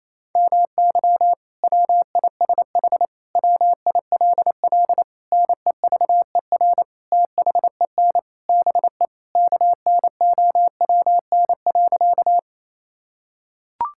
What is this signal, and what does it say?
Morse code, heard as a keyed tone, about 28 wpm, 700 Hz. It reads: MY WISH WILL NEVER THEN BE KNOWN.
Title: Sentences from Top 1600 Words 28wpm